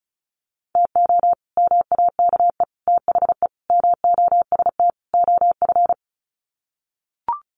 Morse code, heard as a keyed tone, about 34 wpm, 700 Hz. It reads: TO MAKE THE MOST OF
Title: Sets of 5 Words Encore2 35wpm